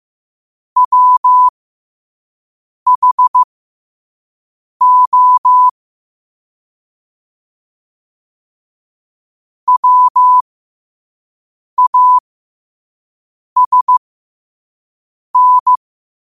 Morse code, heard as a keyed tone, about 15 wpm, spaced as for 5 wpm, 1000 Hz. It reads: WHO WASN
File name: qotdfiles / 5wpm